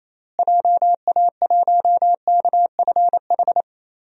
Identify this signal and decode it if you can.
Morse code, heard as a keyed tone, about 27 wpm, 700 Hz. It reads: JA1KFH
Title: DXCC-100 Call Signs - 6 Characters - Repeated 3x - 28wpm